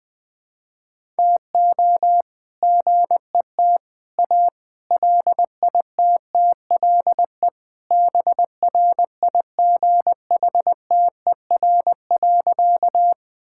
Morse code, heard as a keyed tone, about 20 wpm, 700 Hz. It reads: TO GET A LITTLE BRIGHTER.
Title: All-Age Appropriate Jokes - 20wpm